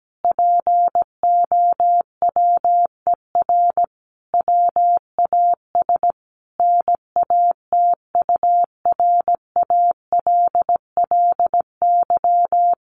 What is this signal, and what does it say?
Morse code, heard as a keyed tone, about 17 wpm, 700 Hz. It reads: POWER WAS NATURALLY
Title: Sentences from Top 1300 Words 17wpm